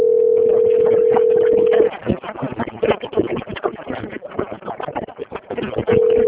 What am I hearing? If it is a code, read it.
Ringback tone (North America), 440 + 480 Hz.
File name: out- - - - - .14020